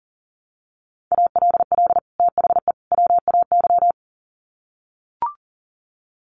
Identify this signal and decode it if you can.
Morse code, heard as a keyed tone, about 40 wpm, 700 Hz. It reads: ALL THE WAY
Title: Sets of 3 Words 40wpm